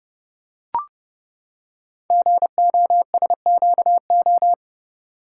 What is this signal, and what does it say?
Morse code, heard as a keyed tone, about 29 wpm, 700 Hz. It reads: GOSQO
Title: Code Groups - 5 Letters - 30wpm